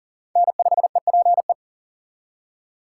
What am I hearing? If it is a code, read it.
Morse code, heard as a keyed tone, about 51 wpm, 700 Hz. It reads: N5EPE